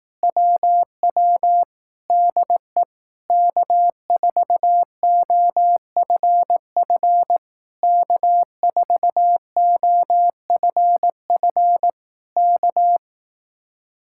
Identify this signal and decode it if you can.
Morse code, heard as a keyed tone, about 18 wpm, 700 Hz. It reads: WW DE K4OFF K4OFF K
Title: Speed-Racing Common QSO Phrases Encore - 15wpm